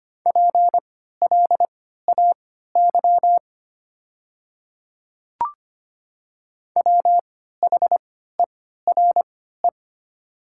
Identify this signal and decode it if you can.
Morse code, heard as a keyed tone, about 25 wpm, spaced as for 14 wpm, 700 Hz. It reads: PLAY WHERE
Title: 3x Character Spacing - Top 200 Words - 25wpm